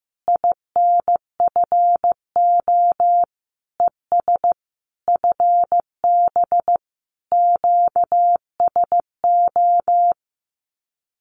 Morse code, heard as a keyed tone, about 15 wpm, 700 Hz. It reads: INFO ES FB QSO
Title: Common QSO Phrases Encore 15wpm